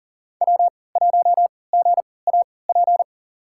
Morse code, heard as a keyed tone, about 40 wpm, spaced as for 23 wpm, 700 Hz. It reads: W1GAP
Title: USA Call Signs - 5 Characters - 3x Character Spacing - 40wpm